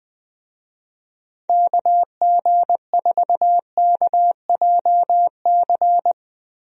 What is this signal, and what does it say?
Morse code, heard as a keyed tone, about 21 wpm, 700 Hz. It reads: KG4KJC
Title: USA Call Signs - 6 Characters - Repeated 2x - 20wpm